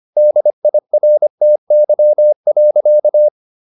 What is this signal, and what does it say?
Morse code, heard as a keyed tone, about 25 wpm, 600 Hz. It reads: DIRTY.